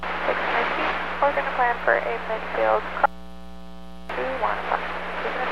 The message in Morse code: none

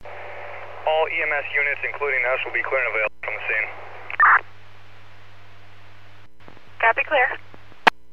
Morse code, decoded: none

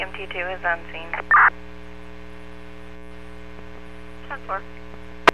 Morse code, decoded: none

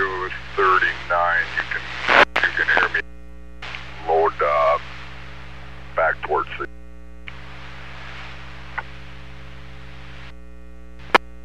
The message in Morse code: none

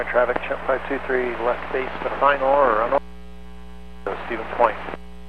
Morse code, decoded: none

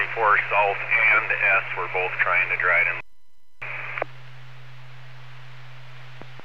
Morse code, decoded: none

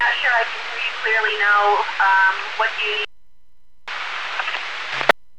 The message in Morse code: none

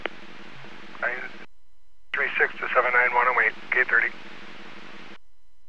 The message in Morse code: none

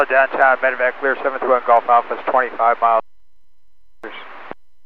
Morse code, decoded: none

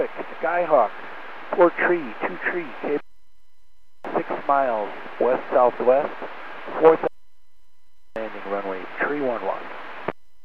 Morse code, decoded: none

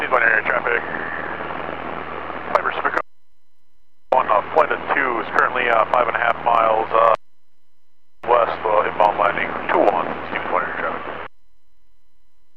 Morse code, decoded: none

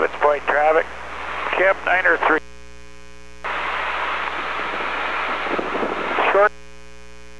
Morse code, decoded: none

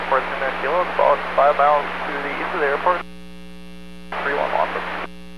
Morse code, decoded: none

